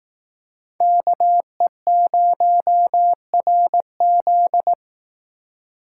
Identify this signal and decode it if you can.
Morse code, heard as a keyed tone, about 18 wpm, 700 Hz. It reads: KE0RZ
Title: Speed Racing - USA Call Signs 15wpm